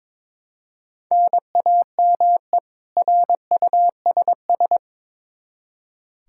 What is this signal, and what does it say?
Morse code, heard as a keyed tone, about 21 wpm, 700 Hz. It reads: NAME RUSS
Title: K1USN SST QSO Practice 22wpm